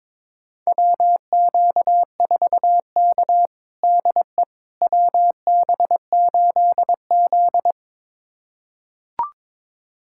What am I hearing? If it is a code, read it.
Morse code, heard as a keyed tone, about 22 wpm, 700 Hz. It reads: WQ4K DE WB8Z
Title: USA Callsign DE Callsign - 4 Characters - 22wpm